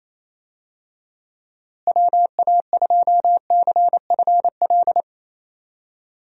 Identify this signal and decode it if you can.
Morse code, heard as a keyed tone, about 28 wpm, 700 Hz. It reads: WA2CFL
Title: USA Call Signs - 6 Characters - Repeated 3x - 28wpm